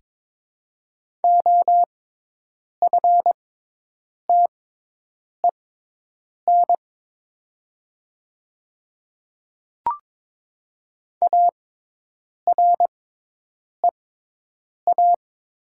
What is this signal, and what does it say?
Morse code, heard as a keyed tone, about 22 wpm, spaced as for 8 wpm, 700 Hz. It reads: OFTEN AREA
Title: 6x Character Spacing - Top 500 Words - 22wpm